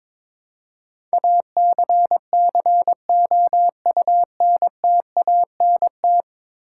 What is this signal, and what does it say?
Morse code, heard as a keyed tone, about 22 wpm, 700 Hz. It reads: ACCOUNTANT
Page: morse-code-ninja-practice-22wpm